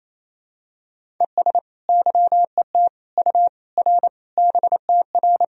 Morse code, heard as a keyed tone, about 28 wpm, 700 Hz. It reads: ES YET U R BTR
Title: Sentences from Top 500 Words and Abbreviations 28wpm